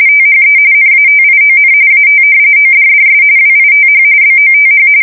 RTTY, about 45.45 baud. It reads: THE NATION AT ODDS WITH BOTH
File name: fsk170 / usnavy1